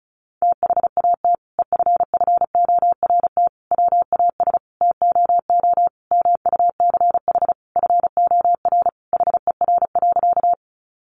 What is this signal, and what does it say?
Morse code, heard as a keyed tone, about 34 wpm, 700 Hz. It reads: THAT EFFORT WAS TOO MUCH FOR HER.
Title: Sentences from Top 1200 Words 35wpm